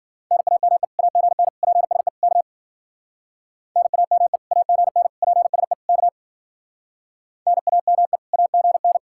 Morse code, heard as a keyed tone, about 50 wpm, 700 Hz. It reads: NAME AGN PSE K NAME AGN PSE K NAME AGN
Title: Speed-Racing Common QSO Phrases Encore - 40wpm